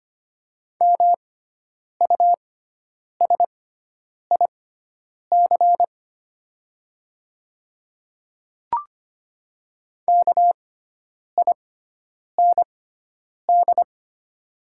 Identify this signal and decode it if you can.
Morse code, heard as a keyed tone, about 25 wpm, spaced as for 9 wpm, 700 Hz. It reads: MUSIC KIND
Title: 6x Character Spacing - Top 500 Words - 25wpm